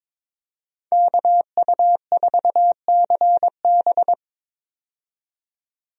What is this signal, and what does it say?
Morse code, heard as a keyed tone, about 22 wpm, 700 Hz. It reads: KU4CB
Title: USA Call Signs - 5 Characters - 22wpm